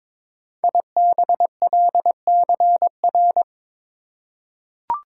Morse code, heard as a keyed tone, about 22 wpm, 700 Hz. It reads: IBLCR
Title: Code Groups - 5 Letter:Numbers - 22wpm